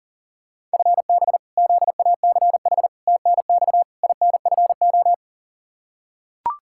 Morse code, heard as a keyed tone, about 40 wpm, 700 Hz. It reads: FB ZACH TNX INFO
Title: K1USN SST QSO Practice 40wpm